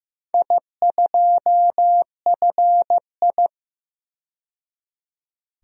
Morse code, heard as a keyed tone, about 15 wpm, 700 Hz. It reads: I2FI
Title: DXCC-100 Call Signs - 4 Characters - 15wpm